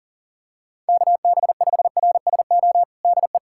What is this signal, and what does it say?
Morse code, heard as a keyed tone, about 40 wpm, 700 Hz. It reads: KB5RSO DE
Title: USA Callsign DE Callsign - 6 Characters - 40wpm